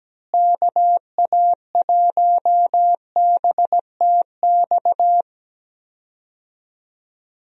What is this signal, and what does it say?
Morse code, heard as a keyed tone, about 17 wpm, 700 Hz. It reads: KA1BTX
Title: USA Call Signs - 6 Characters - Repeated 2x - 17wpm